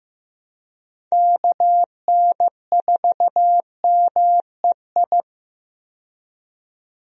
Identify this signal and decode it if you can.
Morse code, heard as a keyed tone, about 15 wpm, 700 Hz. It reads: KN4MEI
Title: USA Call Signs - 6 Characters - 15wpm